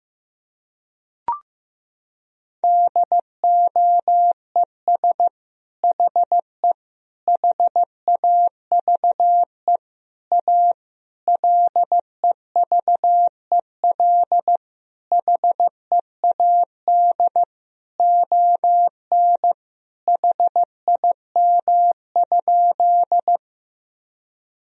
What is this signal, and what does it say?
Morse code, heard as a keyed tone, about 16 wpm, 700 Hz. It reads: DOES HE HAVE A LEVEL HEAD ON HIM?
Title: Sentences from Top 1000 Words - Review Part 4 - 15wpm